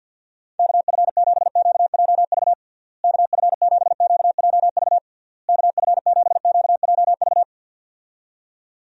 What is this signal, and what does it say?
Morse code, heard as a keyed tone, about 46 wpm, 700 Hz. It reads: KF7QJV KF7QJV KF7QJV
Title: USA Call Signs - 6 Characters - Repeated 3x - 50wpm